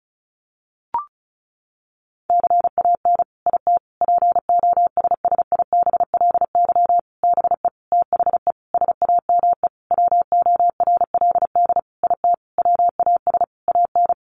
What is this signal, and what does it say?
Morse code, heard as a keyed tone, about 34 wpm, 700 Hz. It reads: CAN IT POSSIBLY BE THE SAME WORLD IT WAS AN
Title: Sentences from Top 1300 Words 35wpm